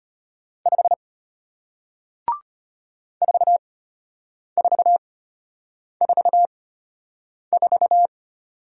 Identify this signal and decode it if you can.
Morse code, heard as a keyed tone, about 33 wpm, spaced as for 8 wpm, 700 Hz. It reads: 5 4444